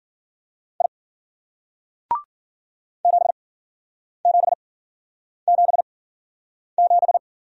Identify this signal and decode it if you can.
Morse code, heard as a keyed tone, about 56 wpm, spaced as for 9 wpm, 700 Hz. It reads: I 7777